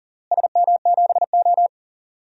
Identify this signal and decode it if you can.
Morse code, heard as a keyed tone, about 40 wpm, 700 Hz. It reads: SM7O
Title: DXCC-100 Call Signs - 4 Characters - Repeated 3x - 40wpm